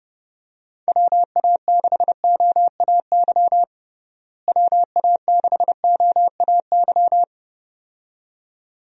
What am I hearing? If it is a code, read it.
Morse code, heard as a keyed tone, about 32 wpm, 700 Hz. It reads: WA6OAY WA6OAY
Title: USA Call Signs - 6 Characters - Repeated 2x - 30wpm